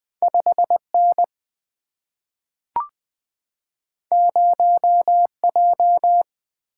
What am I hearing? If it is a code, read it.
Morse code, heard as a keyed tone, about 20 wpm, 700 Hz. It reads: 5N 0J
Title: Single-Letter Single-Number 20wpm